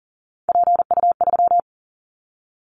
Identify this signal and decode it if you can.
Morse code, heard as a keyed tone, about 40 wpm, 690 Hz. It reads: PU3
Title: Callsign Prefixes for Top 100 DXCC - 40wpm